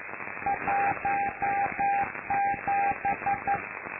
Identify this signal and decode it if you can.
Morse code, heard as a keyed tone, about 11 wpm, 800 Hz, with noise. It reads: 17
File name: lu4aao / Globo Amsat